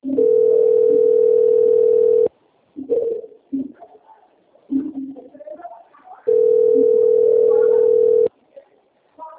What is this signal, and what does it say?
Ringback tone (North America), 440 + 480 Hz, with speech.
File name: out- - - - - .22856